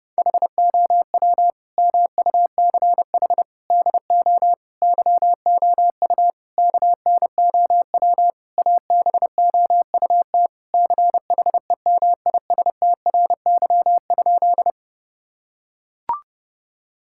Morse code, heard as a keyed tone, about 32 wpm, 700 Hz. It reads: HOW MUCH DO YOU KNOW ABOUT CHEMISTRY?
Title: Sentences from Top 1000 Words - Review Part 6 - 30wpm